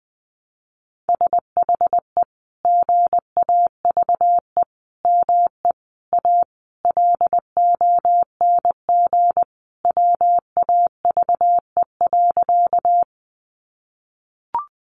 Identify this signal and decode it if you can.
Morse code, heard as a keyed tone, about 20 wpm, 700 Hz. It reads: SHE GAVE ME A LONG WAVE.
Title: Sentences from Top 1400 Words 20wpm